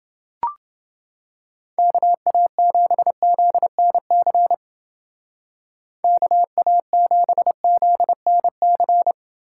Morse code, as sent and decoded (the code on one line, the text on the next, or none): -.- .- --... --.. -. -.-. / -.- .- --... --.. -. -.-.
KA7ZNC KA7ZNC